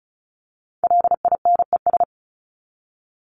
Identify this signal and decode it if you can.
Morse code, heard as a keyed tone, about 34 wpm, 700 Hz. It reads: LINES
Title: ES:S Suffix - Top 100 Words with Base Words 35wpm